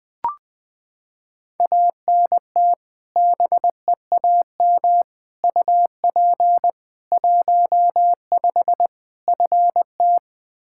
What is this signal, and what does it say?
Morse code, heard as a keyed tone, about 20 wpm, 700 Hz. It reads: ANT BEAM UP 15 FT
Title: Practice QSO - Exchange 2 - Info - 20wpm